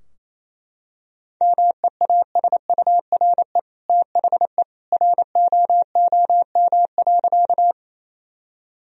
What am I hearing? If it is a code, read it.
Morse code, heard as a keyed tone, about 28 wpm, 700 Hz. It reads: MEASURE THE ROOM.
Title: Sentences from Top 1000 Words - Review Part 2 - 28wpm